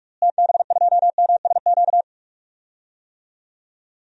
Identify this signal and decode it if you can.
Morse code, heard as a keyed tone, about 41 wpm, 700 Hz. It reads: TB2MSQ
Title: DXCC-100 Call Signs - 6 Characters - Repeated 3x - 45wpm